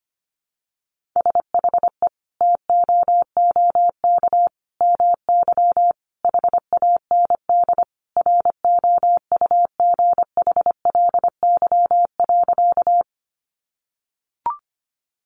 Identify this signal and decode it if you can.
Morse code, heard as a keyed tone, about 25 wpm, 700 Hz. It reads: SHE TOOK MY HAND ROUGHLY.
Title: Sentences from Top 1500 Words 25wpm